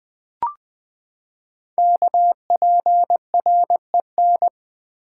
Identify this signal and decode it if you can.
Morse code, heard as a keyed tone, about 20 wpm, 700 Hz. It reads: KPREN
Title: Code Groups - 5 Letters - 20wpm